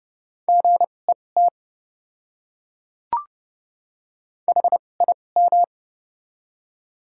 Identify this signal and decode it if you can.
Morse code, heard as a keyed tone, about 29 wpm, spaced as for 21 wpm, 700 Hz. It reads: GET HIM
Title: 2x Character Spacing - Top 200 Words - 30wpm